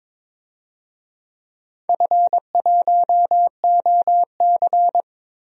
Morse code, heard as a keyed tone, about 22 wpm, 700 Hz. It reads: F1OC
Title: DXCC-100 Call Signs - 4 Characters - 22wpm